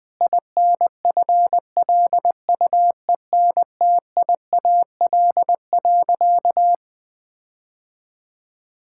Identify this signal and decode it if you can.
Morse code, heard as a keyed tone, about 20 wpm, 700 Hz. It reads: INFLUENTIAL.
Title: Sentences with 4-Word Phrases - 20wpm